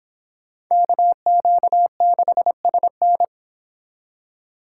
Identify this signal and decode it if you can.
Morse code, heard as a keyed tone, about 29 wpm, 700 Hz. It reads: KQ6SN